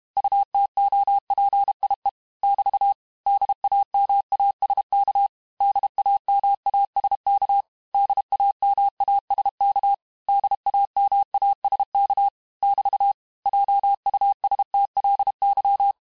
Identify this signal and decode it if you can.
Morse code, heard as a keyed tone, about 32 wpm, 800 Hz. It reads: UTOPIE <BT> DAMASK DAMASK DAMASK DAMASK <BT> JUSTLY
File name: cours-du-mercredi / audios / cours-868-teste-MOTS-vitesse-30mpm